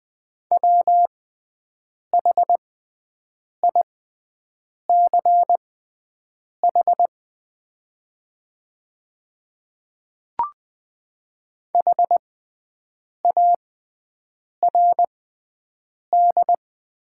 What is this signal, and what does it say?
Morse code, heard as a keyed tone, about 20 wpm, spaced as for 7 wpm, 700 Hz. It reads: WHICH HARD
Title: 6x Character Spacing - Top 300 Words - 20wpm